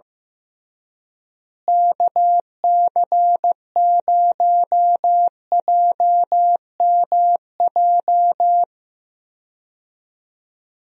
Morse code, heard as a keyed tone, about 15 wpm, 700 Hz. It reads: KC0JMJ
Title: USA Call Signs - 6 Characters - Repeated 3x - 15wpm